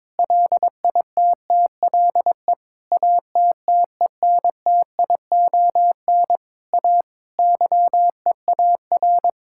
Morse code, heard as a keyed tone, about 22 wpm, 700 Hz. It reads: LITTLE ATTENTION A YEAR
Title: Sentences from Top 1000 Words - Review Part 6 - 22wpm